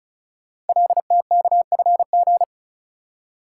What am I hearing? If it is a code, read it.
Morse code, heard as a keyed tone, about 36 wpm, 700 Hz. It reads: LTKFG
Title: Code Groups - 5 Letters - 35wpm